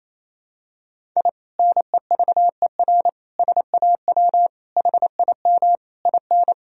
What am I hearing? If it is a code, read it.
Morse code, heard as a keyed tone, about 28 wpm, 700 Hz. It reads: I NEVER SAW HIM IN
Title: Sentences from Top 1000 Words - Review Part 2 - 28wpm